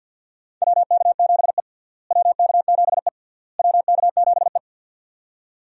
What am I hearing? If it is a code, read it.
Morse code, heard as a keyed tone, about 50 wpm, 700 Hz. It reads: WK7E WK7E WK7E
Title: USA Call Signs - 4 Characters - Repeated 3x - 50wpm